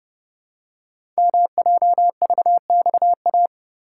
Morse code, heard as a keyed tone, about 30 wpm, 700 Hz. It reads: MJVXA
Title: Code Groups - 5 Letters - 30wpm